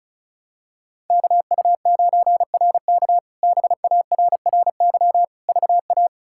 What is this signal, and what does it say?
Morse code, heard as a keyed tone, about 34 wpm, 700 Hz. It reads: KU9RK BARRY VA
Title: K1USN SST QSO Practice - 35wpm